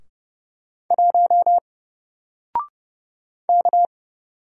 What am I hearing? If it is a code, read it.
Morse code, heard as a keyed tone, about 31 wpm, 700 Hz. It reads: 1 K
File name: Single-Letter-Number-30wpm